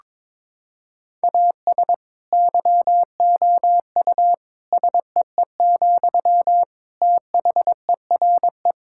Morse code, heard as a keyed tone, about 22 wpm, 700 Hz. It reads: AS YOU SEE, THERE
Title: Sentences from Top 1000 Words - Review Part 6 - 22wpm